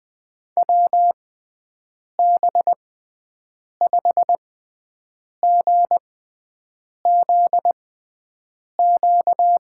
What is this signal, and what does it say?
Morse code, heard as a keyed tone, about 20 wpm, spaced as for 7 wpm, 700 Hz. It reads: WB5GZQ
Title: USA Call Signs - 6 Characters - 6x Character Spacing - 20wpm